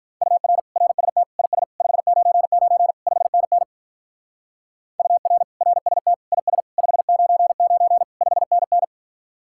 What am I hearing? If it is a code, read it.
Morse code, heard as a keyed tone, about 43 wpm, 700 Hz. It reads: UR RST IS 599 5NN UR RST IS 599 5NN
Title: Speed-Racing Common QSO Phrases Encore - 35wpm